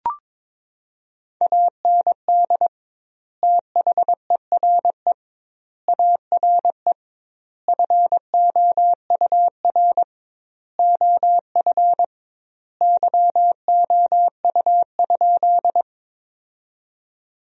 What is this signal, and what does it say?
Morse code, heard as a keyed tone, about 22 wpm, 700 Hz. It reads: AND THERE ARE FOUR OF YOU?
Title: 2x Word-Spacing - Sentences from Top 300 Words - 22wpm